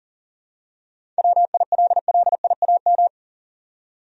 Morse code, heard as a keyed tone, about 40 wpm, 700 Hz. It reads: WILLIAM
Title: USA Common Names - 40wpm